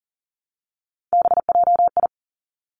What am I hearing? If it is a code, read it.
Morse code, heard as a keyed tone, about 42 wpm, 700 Hz. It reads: BJI
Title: Three Challenging Letter-Number Combinations 40wpm